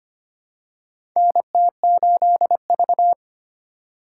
Morse code, heard as a keyed tone, about 25 wpm, 700 Hz. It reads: NT8V
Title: USA Call Signs - 4 Characters - 2x Character Spacing - 25wpm